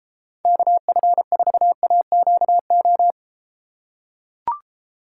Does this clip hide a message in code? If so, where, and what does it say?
Yes: Morse code from 0.5 to 3.1 s, heard as a keyed tone, about 33 wpm, 700 Hz. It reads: KF4AQO